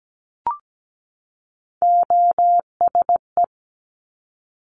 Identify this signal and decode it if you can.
Morse code, heard as a keyed tone, about 17 wpm, 700 Hz. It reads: OSE
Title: Top Three Letter Combinations in Words 17wpm